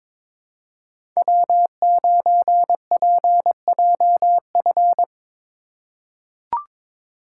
Morse code, heard as a keyed tone, about 22 wpm, 700 Hz. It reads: W9PJF
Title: USA Call Signs - 5 Characters - 3x Character Spacing - 22wpm